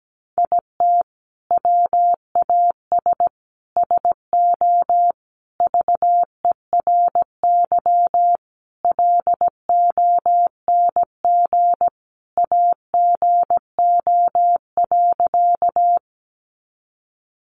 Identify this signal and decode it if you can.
Morse code, heard as a keyed tone, about 17 wpm, 700 Hz. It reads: IT WAS SO VERY LONG AGO.
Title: Sentences from Top 500 Words 17wpm